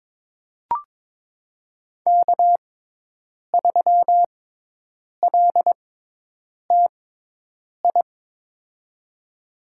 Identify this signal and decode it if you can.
Morse code, heard as a keyed tone, about 22 wpm, spaced as for 8 wpm, 700 Hz. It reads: K3LTI
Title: USA Call Signs - 5 Characters - 6x Character Spacing - 22wpm